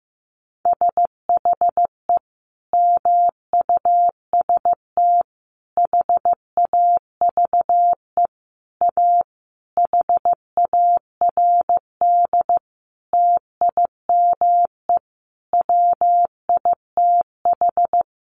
Morse code, heard as a keyed tone, about 15 wpm, 700 Hz. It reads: SHE MUST HAVE A HARD TIME WITH
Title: Sentences from Top 300 Words 15wpm